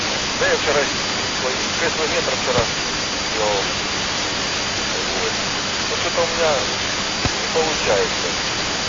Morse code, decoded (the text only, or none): none